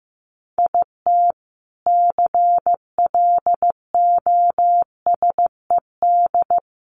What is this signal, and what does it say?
Morse code, heard as a keyed tone, about 15 wpm, 700 Hz. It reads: IT CLOSED
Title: A Sound Of Thunder 15wpm